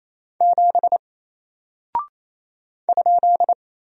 Morse code, heard as a keyed tone, about 28 wpm, 700 Hz. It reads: 7 ?
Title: Single Character (Letter:Number:Punctuation:Prosign) Alt 28wpm